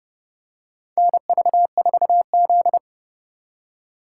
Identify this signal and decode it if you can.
Morse code, heard as a keyed tone, about 30 wpm, 700 Hz. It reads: NV4Z
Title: USA Call Signs - 4 Characters - 2x Character Spacing - 30wpm